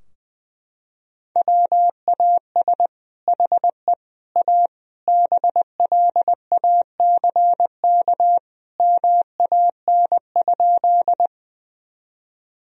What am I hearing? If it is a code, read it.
Morse code, heard as a keyed tone, about 20 wpm, 700 Hz. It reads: WAS HE A BLACK MAN?
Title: Sentences from Top 1000 Words - Review Part 2 - 20wpm